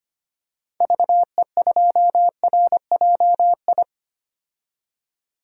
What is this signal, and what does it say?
Morse code, heard as a keyed tone, about 25 wpm, 700 Hz. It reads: VE2RJI
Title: DXCC-100 Call Signs - 6 Characters - 25wpm